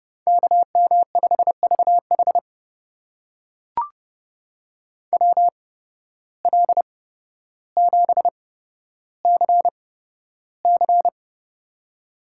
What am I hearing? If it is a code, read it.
Morse code, heard as a keyed tone, about 30 wpm, spaced as for 13 wpm, 700 Hz. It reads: KM5VH WL7CC
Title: USA Call Signs - 5 Characters - 8x Character Spacing - 30wpm